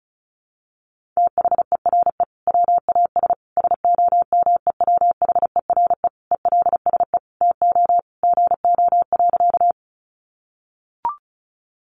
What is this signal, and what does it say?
Morse code, heard as a keyed tone, about 34 wpm, 700 Hz. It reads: THERE WAS SOMEWHERE ELSE TO GO.
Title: Sentences from Top 1400 Words 35wpm